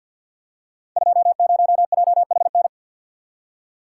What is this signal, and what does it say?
Morse code, heard as a keyed tone, about 53 wpm, 700 Hz. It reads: 20JHN
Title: Code Groups - 5 Letter:Numbers - 50wpm